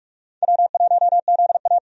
Morse code, heard as a keyed tone, about 48 wpm, 700 Hz. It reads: W1ZA
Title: USA Call Signs - 4 Characters - 45wpm